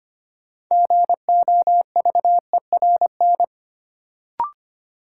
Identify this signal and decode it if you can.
Morse code, heard as a keyed tone, about 25 wpm, 700 Hz. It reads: GOVERN
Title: Sentences from Top 1000 Words - Review Part 2 - New Word Drill - 25wpm